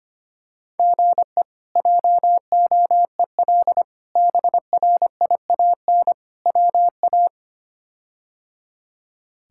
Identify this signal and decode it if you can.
Morse code, heard as a keyed tone, about 25 wpm, 700 Hz. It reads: GE JOEL BRIAN WA
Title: K1USN SST QSO Practice - 25wpm